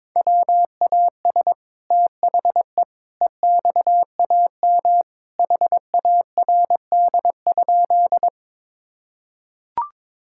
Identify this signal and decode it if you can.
Morse code, heard as a keyed tone, about 22 wpm, 700 Hz. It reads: WAS THE EXAM HARD?
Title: Sentences from Top 1000 Words - Review Part 5 - 22wpm